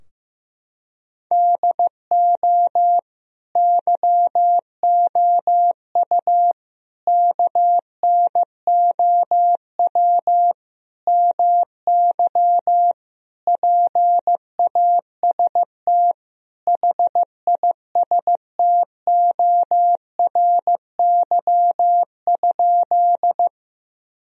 Morse code, heard as a keyed tone, about 15 wpm, 700 Hz. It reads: DO YOU KNOW MY PAST HISTORY?
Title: Sentences from Top 1000 Words - Review Part 3 - 15wpm